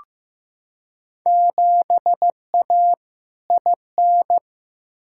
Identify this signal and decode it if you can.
Morse code, heard as a keyed tone, about 15 wpm, 700 Hz. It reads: 7A IN
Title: Field Day Exchanges Encore - 15wpm